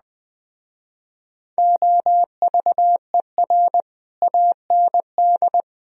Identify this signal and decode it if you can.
Morse code, heard as a keyed tone, about 20 wpm, 700 Hz. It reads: OVER AND
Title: Speed-Racing - Sets of 4 Words 17wpm